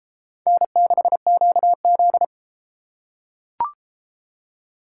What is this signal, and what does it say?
Morse code, heard as a keyed tone, about 31 wpm, 700 Hz. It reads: N6QZ